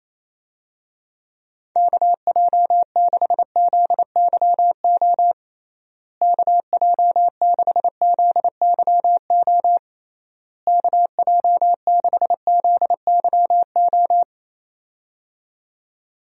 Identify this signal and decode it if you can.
Morse code, heard as a keyed tone, about 27 wpm, 700 Hz. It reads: KJ6ZYO KJ6ZYO KJ6ZYO
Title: USA Call Signs - 6 Characters - Repeated 3x - 28wpm